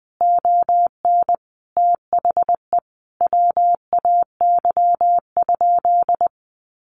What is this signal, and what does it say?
Morse code, heard as a keyed tone, about 20 wpm, 700 Hz. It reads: ON THE WAY?
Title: Sentences from Top 1100 Words 20wpm